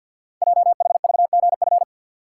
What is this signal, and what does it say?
Morse code, heard as a keyed tone, about 50 wpm, 700 Hz. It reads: JHVGF
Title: Code Groups - 5 Letters - 50wpm